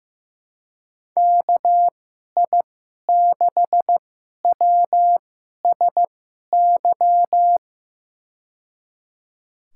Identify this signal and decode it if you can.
Morse code, heard as a keyed tone, about 16 wpm, spaced as for 11 wpm, 700 Hz. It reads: KI6WSY
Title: USA Call Signs - 6 Characters - 2x Character Spacing - 15wpm